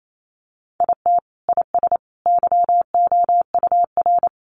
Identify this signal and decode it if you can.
Morse code, heard as a keyed tone, about 28 wpm, 700 Hz. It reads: IT IS YOUR
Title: Sentences from Top 800 Words 28wpm